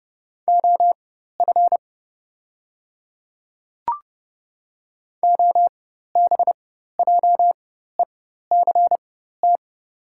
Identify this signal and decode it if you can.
Morse code, heard as a keyed tone, about 30 wpm, spaced as for 14 wpm, 700 Hz. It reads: OF OBJECT
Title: 4x Character Spacing - Top 500 Words - 30wpm